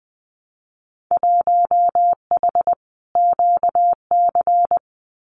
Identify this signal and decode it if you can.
Morse code, heard as a keyed tone, about 20 wpm, 700 Hz. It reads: 1H QC
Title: Winter Field Day Exchanges 20wpm